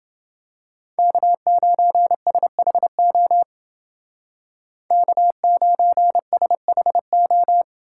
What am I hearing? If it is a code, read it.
Morse code, heard as a keyed tone, about 29 wpm, 700 Hz. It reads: K9SHO K9SHO